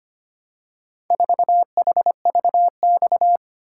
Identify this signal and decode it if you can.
Morse code, heard as a keyed tone, about 25 wpm, 700 Hz. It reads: 4HVX
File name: practice / 25wpm / Four Letter-Number Combinations - 25wpm